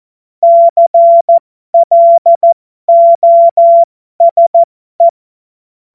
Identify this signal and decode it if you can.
Morse code, heard as a keyed tone, about 14 wpm, spaced as for 12 wpm, 680 Hz. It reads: CLOSE
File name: Morsekurs / Witze